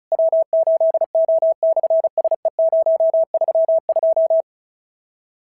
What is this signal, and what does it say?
Morse code, heard as a keyed tone, about 34 wpm, 650 Hz. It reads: W8O/SE032
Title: SOTA QSO Practice 35wpm